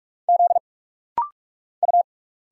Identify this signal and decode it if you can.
Morse code, heard as a keyed tone, about 46 wpm, 690 Hz. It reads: Z U
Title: Single Letters - Rapid-Fire 45wpm